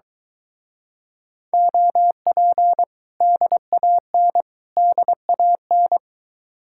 Morse code, heard as a keyed tone, about 23 wpm, 700 Hz. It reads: OP DAN DAN